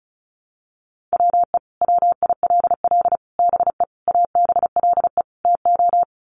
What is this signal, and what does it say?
Morse code, heard as a keyed tone, about 34 wpm, 700 Hz. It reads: WE WILL BE ABLE TO
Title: Sets of 5 Words Encore 35wpm